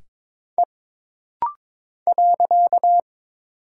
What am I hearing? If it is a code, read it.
Morse code, heard as a keyed tone, about 22 wpm, 700 Hz. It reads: E .
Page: This rapid-fire timing is geared to push your ICR (Instant Character Recognition) skills.